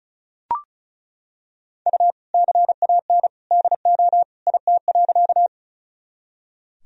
Morse code, heard as a keyed tone, about 36 wpm, 700 Hz. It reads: U CAN DO IT.
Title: Sentences from Top 100 Words and Abbreviations 35wpm